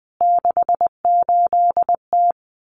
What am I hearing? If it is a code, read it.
Morse code, heard as a keyed tone, about 20 wpm, 700 Hz. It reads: 68T
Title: CWT Exchange 20wpm